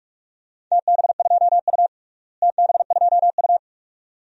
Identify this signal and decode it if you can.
Morse code, heard as a keyed tone, about 42 wpm, 700 Hz. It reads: TB2U TB2U
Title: DXCC-100 Call Signs - 4 Characters - Repeated 2x - 45wpm